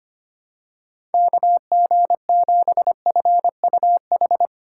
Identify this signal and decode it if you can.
Morse code, heard as a keyed tone, about 25 wpm, 700 Hz. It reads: KG7FUH
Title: USA Call Signs - 6 Characters - 4x Character Spacing - 25wpm